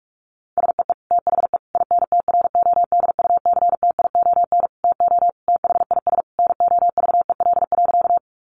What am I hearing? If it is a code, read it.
Morse code, heard as a keyed tone, about 46 wpm, 700 Hz. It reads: SEE THE INTRODUCTION TO THIS NOVEL.
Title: Sentences from Top 1300 Words 45wpm